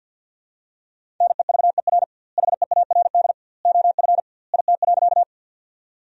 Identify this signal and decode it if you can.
Morse code, heard as a keyed tone, about 50 wpm, 700 Hz. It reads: NEVER HEARD OF IT.
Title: 2x Word-Spacing - Sentences from Top 400 Words - 50wpm